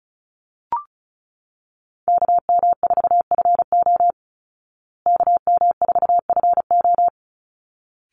Morse code, heard as a keyed tone, about 34 wpm, 700 Hz. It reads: KM4FO KM4FO
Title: Call Signs Repeated 3x - 35wpm